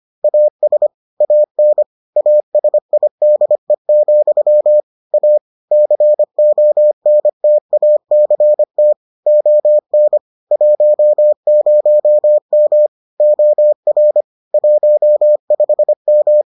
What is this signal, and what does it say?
Morse code, heard as a keyed tone, about 25 wpm, 600 Hz. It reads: AS AN ASIDE, A CONTACT ON 10M OR 15M